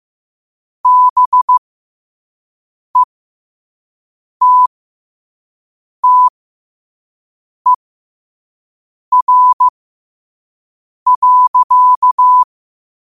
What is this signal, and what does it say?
Morse code, heard as a keyed tone, about 15 wpm, spaced as for 5 wpm, 1000 Hz. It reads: BETTER.
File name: 5wpm